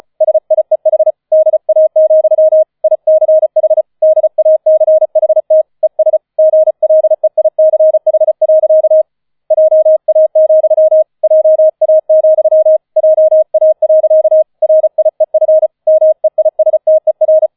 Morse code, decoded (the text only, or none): SIEH DA, ICH DACHT ES GLEICH. JA, JA, JA. RIEF MEISTER